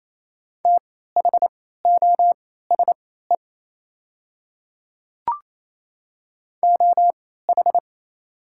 Morse code, heard as a keyed tone, about 28 wpm, spaced as for 16 wpm, 690 Hz. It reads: THOSE OH
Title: 3x Character Spacing - Top 500 Words - 28wpm